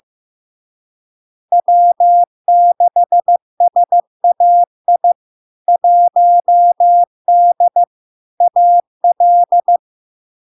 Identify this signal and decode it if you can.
Morse code, heard as a keyed tone, about 15 wpm, 700 Hz. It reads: W6SAI 1D AL
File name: LARK-FD-2022-Log-in-CW-15wpm-700-Hz